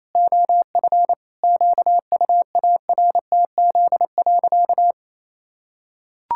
Morse code, heard as a keyed tone, about 27 wpm, 700 Hz. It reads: OF QUARTZ.
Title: Pangram Challenge v2 - 28wpm